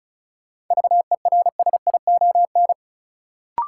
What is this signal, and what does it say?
Morse code, heard as a keyed tone, about 36 wpm, 700 Hz. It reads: VERSION
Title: Sentences from Top 1400 Words - New Word Drill - 35wpm